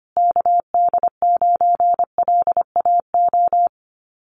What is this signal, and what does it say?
Morse code, heard as a keyed tone, about 25 wpm, 700 Hz. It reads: KD9LAO
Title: Common QSO Phrases Encore 25wpm